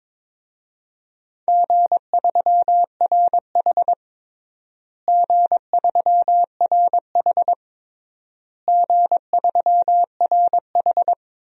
Morse code, heard as a keyed tone, about 22 wpm, 700 Hz. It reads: G3RH G3RH G3RH
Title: DXCC-100 Call Signs - 4 Characters - Repeated 3x - 22wpm